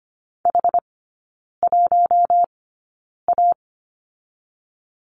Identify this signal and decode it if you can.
Morse code, heard as a keyed tone, about 25 wpm, spaced as for 9 wpm, 700 Hz. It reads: H1A
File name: Lesson 15e - Number 5 - Three Characters 25wpm